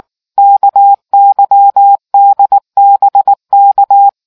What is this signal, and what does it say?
Morse code, heard as a keyed tone, about 19 wpm, 800 Hz. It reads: KYDBK